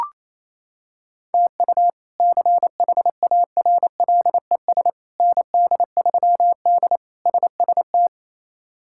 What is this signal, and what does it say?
Morse code, heard as a keyed tone, about 28 wpm, 700 Hz. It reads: TU CHARLES ND3D SST
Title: K1USN SST QSO Practice - 28wpm